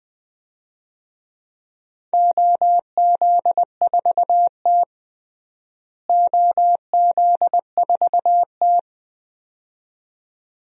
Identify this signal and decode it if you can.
Morse code, heard as a keyed tone, about 20 wpm, 700 Hz. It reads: OZ4T OZ4T
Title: DXCC-100 Call Signs - 4 Characters - Repeated 2x - 20wpm